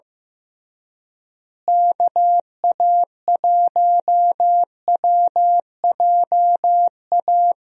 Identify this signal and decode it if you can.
Morse code, heard as a keyed tone, about 15 wpm, 700 Hz. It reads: KA1WJA
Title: USA Call Signs - 6 Characters - Repeated 3x - 15wpm